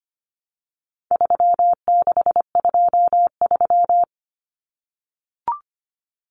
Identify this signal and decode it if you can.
Morse code, heard as a keyed tone, about 25 wpm, 700 Hz. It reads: 3623
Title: Numbers - 4-Digit 25wpm